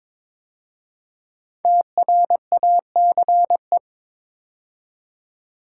Morse code, heard as a keyed tone, about 22 wpm, 700 Hz. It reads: TRACE
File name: 5 Letter Words - First 3 Letters Given - 22wpm